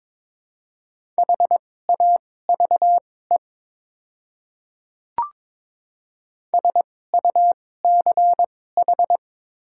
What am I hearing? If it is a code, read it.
Morse code, heard as a keyed tone, about 22 wpm, spaced as for 16 wpm, 700 Hz. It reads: HAVE SUCH
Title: 2x Character Spacing - Top 200 Words - 22wpm